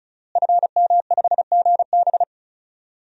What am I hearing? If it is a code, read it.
Morse code, heard as a keyed tone, about 34 wpm, 700 Hz. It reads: FM5GB
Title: DXCC-100 Call Signs - 5 Characters - Repeated 3x - 35wpm